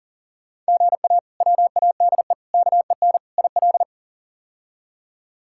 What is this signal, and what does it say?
Morse code, heard as a keyed tone, about 37 wpm, 700 Hz. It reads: GA WADE KEN IL
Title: K1USN SST QSO Practice - 40wpm